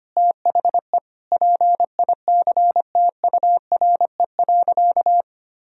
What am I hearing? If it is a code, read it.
Morse code, heard as a keyed tone, about 25 wpm, 700 Hz. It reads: THE PICTURE.
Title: Sentences from Top 1000 Words - Review Part 1 - 25wpm